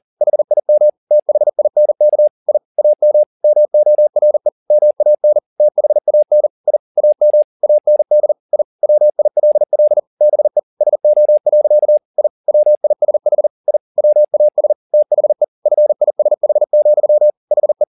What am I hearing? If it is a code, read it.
Morse code, heard as a keyed tone, about 40 wpm, 600 Hz. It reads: HIM THINK I AM MORE MAN THAN I AM AND I WILL BE SO. I WISH I WAS THE FISH, HE